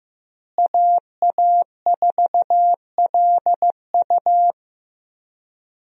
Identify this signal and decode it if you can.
Morse code, heard as a keyed tone, about 15 wpm, 700 Hz. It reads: AA4LU
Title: USA Call Signs - 5 Characters - Repeated 3x - 15wpm